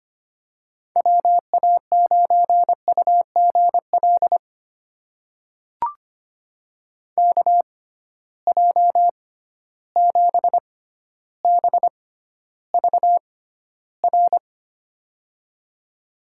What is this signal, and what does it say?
Morse code, heard as a keyed tone, about 25 wpm, spaced as for 13 wpm, 700 Hz. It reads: WA9UGL KJ7BVR
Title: USA Call Signs - 6 Characters - 6x Character Spacing - 25wpm